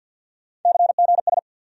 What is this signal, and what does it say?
Morse code, heard as a keyed tone, about 48 wpm, 690 Hz. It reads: CGS
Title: USA Call Sign Postfixes - 3 Letters 50wpm